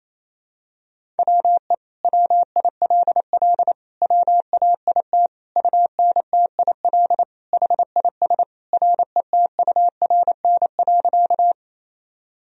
Morse code, heard as a keyed tone, about 28 wpm, 700 Hz. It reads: WE WILL WAIT UNTIL HIS RETURN.
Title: Sentences from Top 1000 Words - Review Part 1 - 28wpm